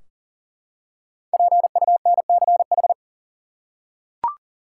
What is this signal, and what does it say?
Morse code, heard as a keyed tone, about 40 wpm, 700 Hz. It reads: PUNCH
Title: Five Letter Words - 40wpm